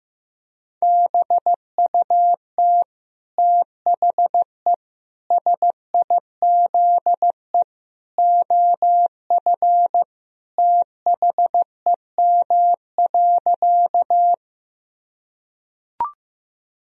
Morse code, heard as a keyed tone, about 16 wpm, 700 Hz. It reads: BUT THE SIZE OF THEM.
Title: Sentences from Top 1000 Words - Review Part 2 - 15wpm